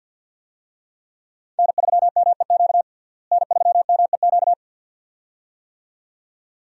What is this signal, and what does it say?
Morse code, heard as a keyed tone, about 51 wpm, 700 Hz. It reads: N3MEQ N3MEQ
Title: USA Call Signs - 5 Characters - Repeated 2x - 50wpm